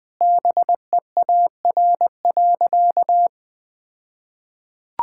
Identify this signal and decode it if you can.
Morse code, heard as a keyed tone, about 20 wpm, 700 Hz. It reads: BEAR.
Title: All-Age Appropriate Jokes - 20wpm